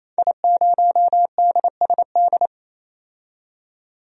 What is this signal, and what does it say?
Morse code, heard as a keyed tone, about 28 wpm, 700 Hz. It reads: I0DSD
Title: DXCC-100 Call Signs - 5 Characters - 28wpm